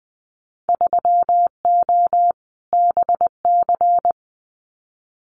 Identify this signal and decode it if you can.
Morse code, heard as a keyed tone, about 20 wpm, 700 Hz. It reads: 3O BC
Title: Winter Field Day Exchanges Encore 20wpm